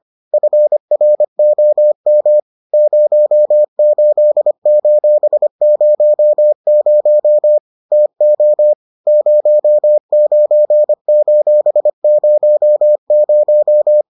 Morse code, heard as a keyed tone, about 25 wpm, 600 Hz. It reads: FROM 08:00 TO 09:00